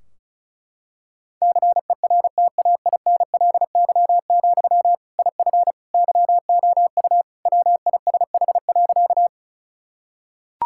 Morse code, heard as a keyed tone, about 36 wpm, 700 Hz. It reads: CERTAINLY, IF YOU WISH.
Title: Sentences from Top 1600 Words 35wpm